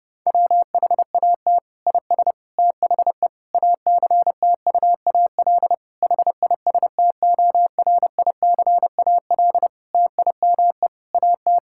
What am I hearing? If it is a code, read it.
Morse code, heard as a keyed tone, about 30 wpm, 700 Hz. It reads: WHAT IS THE ACTUAL HISTORICAL TIME AT
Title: Sentences from Top 1000 Words - Review Part 5 - 30wpm